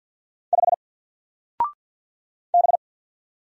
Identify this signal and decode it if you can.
Morse code, heard as a keyed tone, about 50 wpm, 700 Hz. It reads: 5 B
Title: Single Letter-Number 50wpm